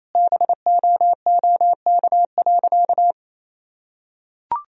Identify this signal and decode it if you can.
Morse code, heard as a keyed tone, about 28 wpm, 700 Hz. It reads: BOOK.
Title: Sentences from Top 1000 Words - Review Part 3 - 28wpm